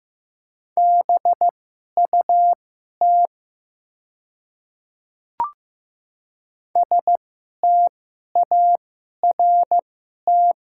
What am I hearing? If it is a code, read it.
Morse code, heard as a keyed tone, about 15 wpm, spaced as for 11 wpm, 690 Hz. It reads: BUT START
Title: 2x Character Spacing - Top 500 Words - 15wpm